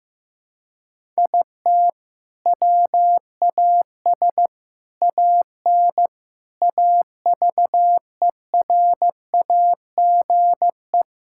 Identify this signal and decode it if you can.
Morse code, heard as a keyed tone, about 16 wpm, 700 Hz. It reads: IT WAS AN AVERAGE
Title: Sentences from Top 1000 Words - Review Part 6 - 15wpm